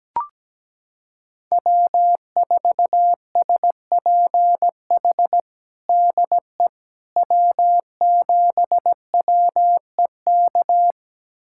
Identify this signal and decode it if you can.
Morse code, heard as a keyed tone, about 17 wpm, 700 Hz. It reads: W4SPH DE W7WEK
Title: USA Callsign DE Callsign - 5 Characters - 17wpm